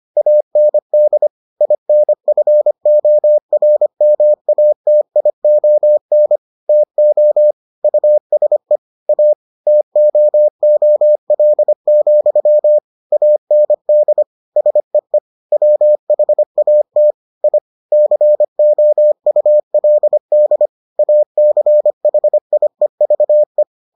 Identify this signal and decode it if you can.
Morse code, heard as a keyed tone, about 25 wpm, 600 Hz. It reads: AND INFORMATION TO USE A TOOL, AND SEE WHAT I COULD ACHIEVE